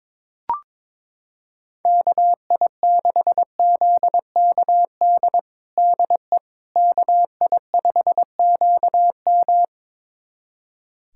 Morse code, heard as a keyed tone, about 22 wpm, 700 Hz. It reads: KI6ZKD DE KI5QM
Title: USA Callsign DE Callsign 22wpm